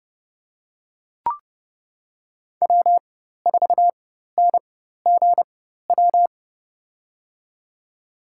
Morse code, heard as a keyed tone, about 32 wpm, spaced as for 14 wpm, 700 Hz. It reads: W4NGW
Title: USA Call Signs - 5 Characters - 4x Character Spacing - 30wpm